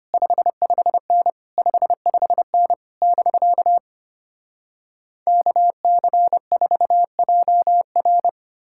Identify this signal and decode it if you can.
Morse code, heard as a keyed tone, about 33 wpm, 700 Hz. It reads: 55N 55N <BK> KC4JR